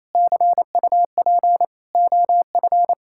Morse code, heard as a keyed tone, about 28 wpm, 700 Hz. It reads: CUP OF
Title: Sentences from Top 1600 Words 28wpm